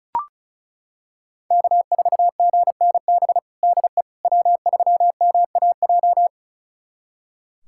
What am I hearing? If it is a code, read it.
Morse code, heard as a keyed tone, about 34 wpm, 700 Hz. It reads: K4GNB DE W3MAJ
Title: USA Callsign DE Callsign - 5 Characters - 35wpm